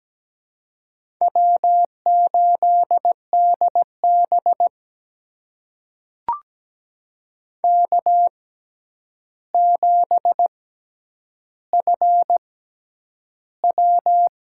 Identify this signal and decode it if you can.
Morse code, heard as a keyed tone, about 17 wpm, spaced as for 9 wpm, 700 Hz. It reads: W8DB K7FW
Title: USA Call Signs - 4 Characters - 6x Character Spacing - 17wpm